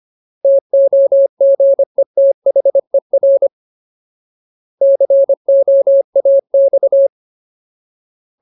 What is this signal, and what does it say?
Morse code, heard as a keyed tone, about 25 wpm, 550 Hz. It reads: TOGETHER COAX
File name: Ultimate Word List / 25WPM Ultimate